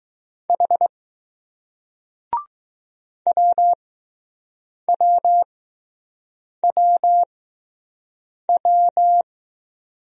Morse code, heard as a keyed tone, about 20 wpm, spaced as for 6 wpm, 700 Hz. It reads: H WWWW